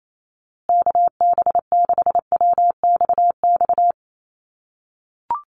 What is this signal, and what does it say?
Morse code, heard as a keyed tone, about 28 wpm, 700 Hz. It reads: KB6WXX
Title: USA Call Signs Repeated 2x 28wpm